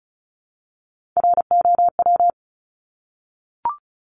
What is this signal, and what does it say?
Morse code, heard as a keyed tone, about 34 wpm, 700 Hz. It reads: ROW
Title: Top 1751-2000 Words 35wpm